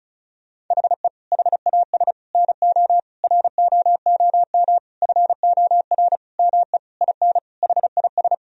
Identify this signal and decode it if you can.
Morse code, heard as a keyed tone, about 36 wpm, 700 Hz. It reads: HE HAS NO ROOM FOR ME IN HIS
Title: Sentences from Top 1000 Words - Review Part 4 - 35wpm